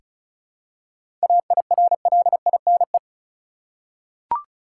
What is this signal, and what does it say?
Morse code, heard as a keyed tone, about 36 wpm, 700 Hz. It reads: AIRLINE
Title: Seven Letter Words - 35wpm